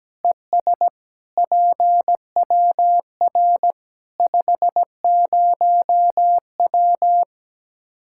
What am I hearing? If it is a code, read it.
Morse code, heard as a keyed tone, about 17 wpm, 700 Hz. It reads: ES PWR 50W
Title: Practice QSO - Exchange 2 - Info - 17wpm